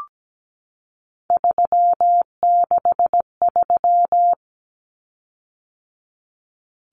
Morse code, heard as a keyed tone, about 17 wpm, 700 Hz. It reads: 363
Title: Common Cut Numbers - 3-Digit 17wpm